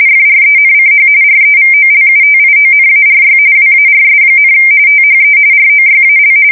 RTTY, about 45.45 baud. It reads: FRENCH REVOLUTIONARY WARS, THE ROYAL
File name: fsk170 / usnavy1